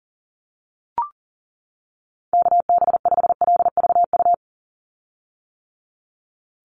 Morse code, heard as a keyed tone, about 40 wpm, 700 Hz. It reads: KB5LVU
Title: USA Call Signs 40wpm